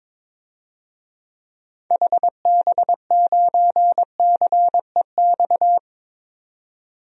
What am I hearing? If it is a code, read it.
Morse code, heard as a keyed tone, about 22 wpm, 700 Hz. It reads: HB9CEX
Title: DXCC-100 Call Signs - 6 Characters - 22wpm